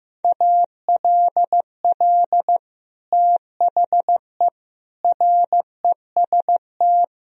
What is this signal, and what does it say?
Morse code, heard as a keyed tone, about 16 wpm, 700 Hz. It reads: ALL THE REST
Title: Sentences from Top 1000 Words - Review Part 4 - 15wpm